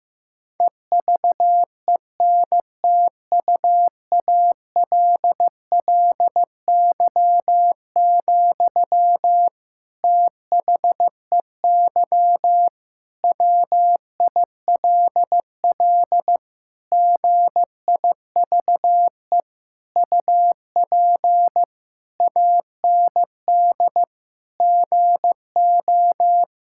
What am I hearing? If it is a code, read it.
Morse code, heard as a keyed tone, about 16 wpm, 700 Hz. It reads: EVENTUALLY, THEY WILL GIVE UP AND GO
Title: Sentences from Top 1000 Words - Review Part 6 - 15wpm